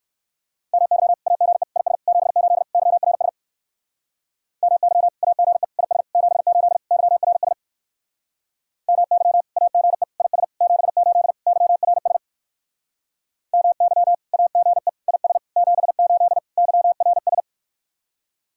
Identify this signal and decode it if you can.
Morse code, heard as a keyed tone, about 58 wpm, 700 Hz. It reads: MY AGE IS 78 YRS MY AGE IS 78 YRS MY AGE IS 78 YRS MY AGE IS 78 YRS